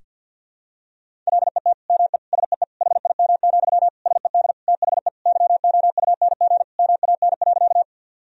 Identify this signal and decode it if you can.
Morse code, heard as a keyed tone, about 50 wpm, 700 Hz. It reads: LET ME SEE HIM, SED THE YOUNG MAN.
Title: Sentences from Top 400 Words and Abbreviations 50wpm